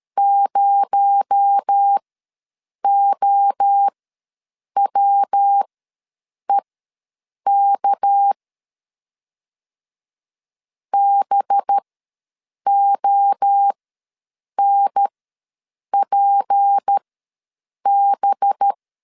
Morse code, heard as a keyed tone, about 13 wpm, spaced as for 7 wpm, 800 Hz. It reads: 0OWEK BONPB
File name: LEZIONI / 7 GF9Z / 7 GF ws 12wpm 3140